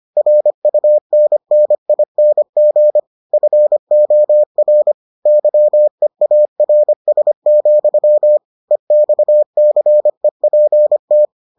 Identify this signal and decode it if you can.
Morse code, heard as a keyed tone, about 25 wpm, 600 Hz. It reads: RUNNING FOR YEARS, EXCEPT